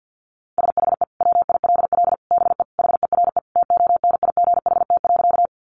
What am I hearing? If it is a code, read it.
Morse code, heard as a keyed tone, about 50 wpm, 700 Hz. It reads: SHE WILL BE HERE TONIGHT.
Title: Sentences from Top 600 Words 50wpm